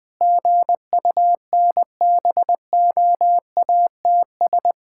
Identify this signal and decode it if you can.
Morse code, heard as a keyed tone, about 20 wpm, 700 Hz. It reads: GUNBOATS
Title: Pangram Challenge v2 - 20wpm